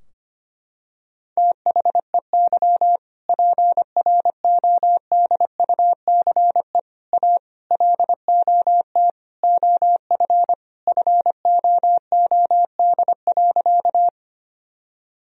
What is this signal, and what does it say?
Morse code, heard as a keyed tone, about 25 wpm, 700 Hz. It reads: THEY PRODUCE A LOT OF FOOD.
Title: Sentences from Top 1000 Words - Review Part 3 - 25wpm